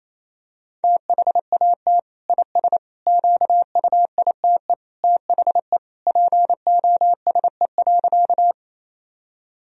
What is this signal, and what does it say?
Morse code, heard as a keyed tone, about 28 wpm, 700 Hz. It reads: THAT IS QUITE THE POSE.
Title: Sentences from Top 1000 Words - Review Part 2 - 28wpm